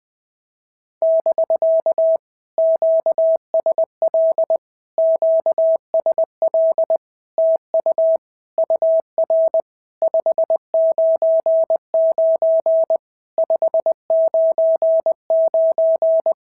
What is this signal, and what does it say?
Morse code, heard as a keyed tone, about 20 wpm, 650 Hz. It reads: <BK> QSL QSL TU UR 599 599
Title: POTA QSO Practice - 20wpm